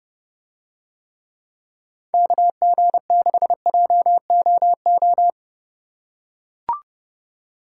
Morse code, heard as a keyed tone, about 30 wpm, 700 Hz. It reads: KG6JOO